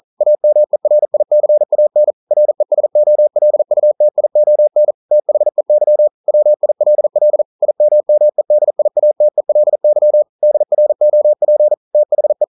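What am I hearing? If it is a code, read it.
Morse code, heard as a keyed tone, about 42 wpm, 600 Hz. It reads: AMERICAN RESOLUTION THEY WILL IMMEDIATELY DROP THE
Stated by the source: Morse code at 40 wpm, practice text, CW med Gnister.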